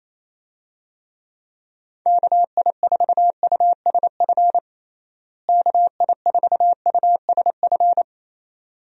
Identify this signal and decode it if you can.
Morse code, heard as a keyed tone, about 28 wpm, 700 Hz. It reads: KI4USF KI4USF
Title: USA Call Signs - 6 Characters - Repeated 2x - 28wpm